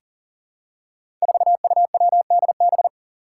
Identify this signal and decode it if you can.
Morse code, heard as a keyed tone, about 43 wpm, 700 Hz. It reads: 4UWDB
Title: Five Challenging Letter-Number Combinations 40wpm